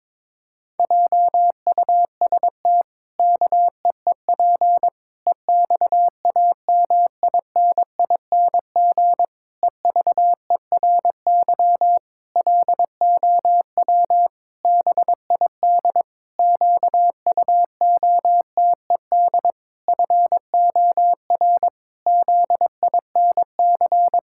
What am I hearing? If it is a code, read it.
Morse code, heard as a keyed tone, about 22 wpm, 700 Hz. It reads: JUST KEEP EXAMINING EVERY LOW BID QUOTED FOR ZINC
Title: Pangram Challenge v2 - 22wpm